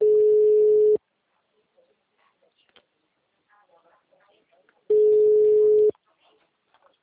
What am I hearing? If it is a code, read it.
Ringback tone (Europe, 425 Hz).